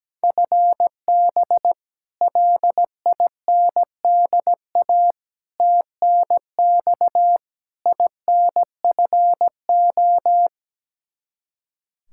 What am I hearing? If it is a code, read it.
Morse code, heard as a keyed tone, about 17 wpm, 700 Hz. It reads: FB LINDA TNX INFO
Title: Practice QSO - Exchange 3 - More Info - 17wpm